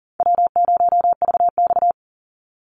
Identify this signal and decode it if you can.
Morse code, heard as a keyed tone, about 40 wpm, 700 Hz. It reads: W0VX
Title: Call Signs - 40wpm